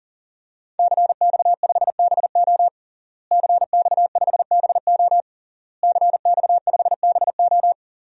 Morse code, heard as a keyed tone, about 40 wpm, 700 Hz. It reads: CX5BO CX5BO CX5BO
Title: DXCC-100 Call Signs - 5 Characters - Repeated 3x - 40wpm